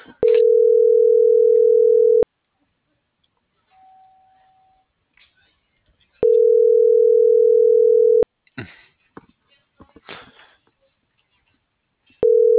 Ringback tone (North America), 440 + 480 Hz.
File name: CENTRAL SAGOT / out anyelo1.sagot - .16140